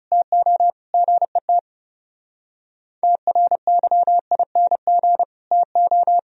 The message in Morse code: TO GET TRYING TO

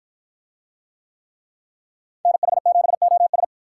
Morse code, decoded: NH7OS

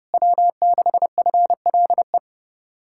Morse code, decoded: W6FLE

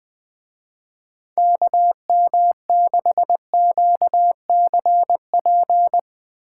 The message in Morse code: KM6QCP

